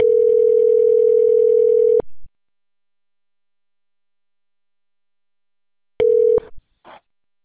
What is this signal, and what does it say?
Ringback tone (North America), 440 + 480 Hz.